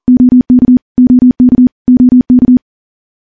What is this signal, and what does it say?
Morse code, heard as a keyed tone, about 39 wpm, 250 Hz. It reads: OK OK OK
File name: morse / morse-training-40WPM